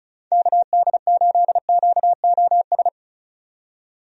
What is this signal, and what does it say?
Morse code, heard as a keyed tone, about 36 wpm, 700 Hz. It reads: KD8QOS
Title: USA Call Signs - 6 Characters - 35wpm